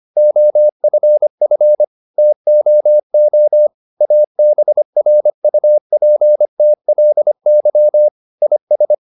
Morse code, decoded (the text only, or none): OFF TOO ABRUPTLY IS